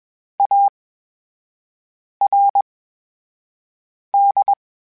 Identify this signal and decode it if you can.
Morse code, heard as a keyed tone, about 21 wpm, spaced as for 5 wpm, 800 Hz. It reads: ARD
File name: Special INGs-mp3s / Special INGs-mp3s / VerySlowToFaster Ing @20